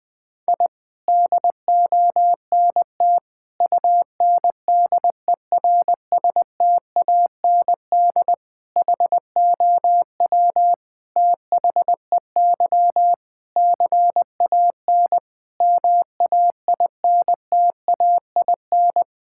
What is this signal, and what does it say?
Morse code, heard as a keyed tone, about 21 wpm, 700 Hz. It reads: I DONT UNDERSTAND HOW THEY CAN MAINTAIN
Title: Sentences from Top 1000 Words - Review Part 5 - 20wpm